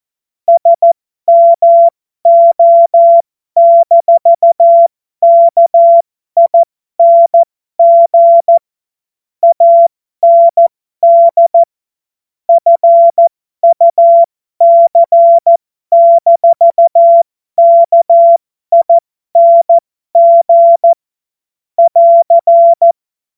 Morse code, heard as a keyed tone, about 14 wpm, spaced as for 12 wpm, 680 Hz. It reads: SMO-KING AND FUC-KING <AR>
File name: Morsekurs / Witze